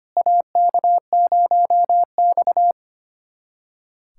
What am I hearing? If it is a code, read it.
Morse code, heard as a keyed tone, about 25 wpm, 700 Hz. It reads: AK0X
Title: USA Callsign DE Callsign - 4 Characters - 25wpm